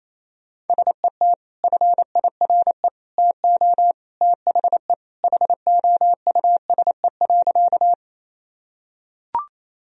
Morse code, heard as a keyed tone, about 28 wpm, 700 Hz. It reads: SET FIRE TO THE HOUSE.
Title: Sentences from Top 1000 Words - Review Part 1 - 28wpm